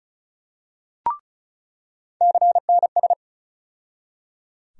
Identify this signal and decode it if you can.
Morse code, heard as a keyed tone, about 36 wpm, 700 Hz. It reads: CNS
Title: USA Call Sign Postfixes - 3 Letters 35wpm